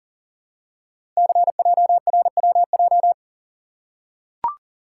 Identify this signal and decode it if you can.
Morse code, heard as a keyed tone, about 39 wpm, 700 Hz. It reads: CJRWJ
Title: Code Groups - 5 Letters - 40wpm